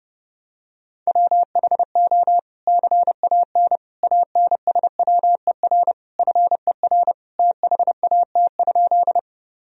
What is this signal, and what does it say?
Morse code, heard as a keyed tone, about 30 wpm, 700 Hz. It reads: WHO CAN ANSWER FER THAT?
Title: Sentences from Top 300 Words and Abbreviations 30wpm